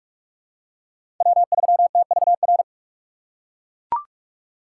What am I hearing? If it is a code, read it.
Morse code, heard as a keyed tone, about 42 wpm, 700 Hz. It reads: W3TVR
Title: USA Call Signs - 5 Characters - 4x Character Spacing - 45wpm